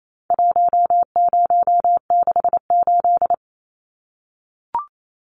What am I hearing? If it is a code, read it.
Morse code, heard as a keyed tone, about 28 wpm, 700 Hz. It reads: 1068
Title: Numbers - 4-Digit 28wpm